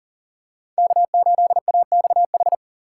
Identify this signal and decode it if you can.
Morse code, heard as a keyed tone, about 40 wpm, 700 Hz. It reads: K8AXH
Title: USA Call Signs - 5 Characters - 2x Character Spacing - 40wpm